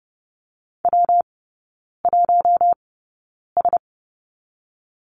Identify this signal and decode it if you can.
Morse code, heard as a keyed tone, about 30 wpm, 700 Hz. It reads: W 1 S
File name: Lesson 19e - Letter w - Three Characters 30wpm